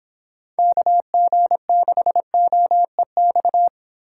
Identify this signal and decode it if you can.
Morse code, heard as a keyed tone, about 25 wpm, 700 Hz. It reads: KG6OEX